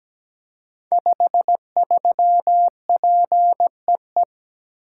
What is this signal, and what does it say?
Morse code, heard as a keyed tone, about 17 wpm, 700 Hz. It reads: 53PEE
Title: Code Groups - 5 Letter:Numbers - 17wpm